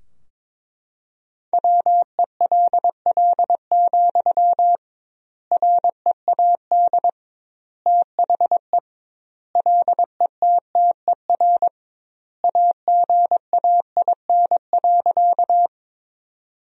Morse code, heard as a keyed tone, about 22 wpm, 700 Hz. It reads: WELL, READ THE LETTER AGAIN.
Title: 2x Word-Spacing - Sentences from Top 300 Words - 22wpm